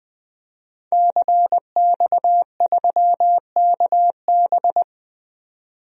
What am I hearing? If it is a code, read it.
Morse code, heard as a keyed tone, about 20 wpm, 700 Hz. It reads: CX3KB
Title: DXCC-100 Call Signs - 5 Characters - 20wpm